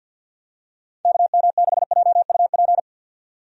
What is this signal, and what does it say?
Morse code, heard as a keyed tone, about 50 wpm, 700 Hz. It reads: KM6JUP
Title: USA Call Signs - 6 Characters - 50wpm